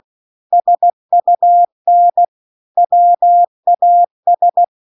Morse code, heard as a keyed tone, about 16 wpm, 700 Hz. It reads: SUN WAS
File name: 16WPM